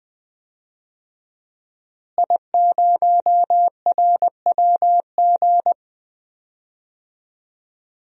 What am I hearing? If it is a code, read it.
Morse code, heard as a keyed tone, about 20 wpm, 700 Hz. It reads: I0RWG
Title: DXCC-100 Call Signs - 5 Characters - Repeated 2x - 20wpm